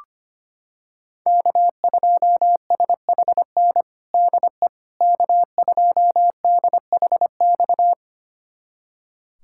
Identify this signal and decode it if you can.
Morse code, heard as a keyed tone, about 25 wpm, 700 Hz. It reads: K2SHN DE K2DHX
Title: USA Callsign DE Callsign - 5 Characters - 25wpm